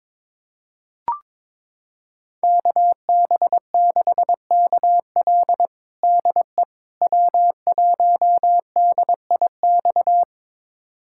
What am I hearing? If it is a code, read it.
Morse code, heard as a keyed tone, about 22 wpm, 700 Hz. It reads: KB6KL DE W1DIX
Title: USA Callsign DE Callsign 22wpm